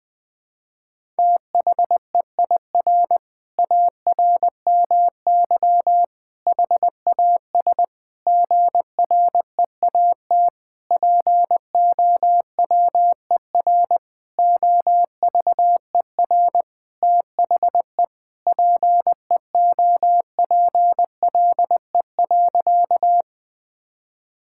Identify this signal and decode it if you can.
Morse code, heard as a keyed tone, about 20 wpm, 700 Hz. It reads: THEIR ARMY HAS GREAT POWER OVER THE PEOPLE.
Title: Sentences from Top 1000 Words - Review Part 5 - 20wpm